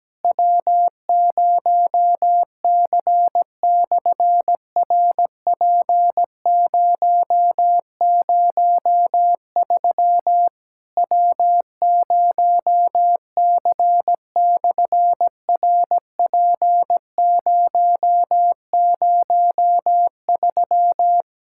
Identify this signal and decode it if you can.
Morse code, heard as a keyed tone, about 17 wpm, 700 Hz. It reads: W0C/RP003 W0C/RP003
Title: SOTA QSO Practice 17wpm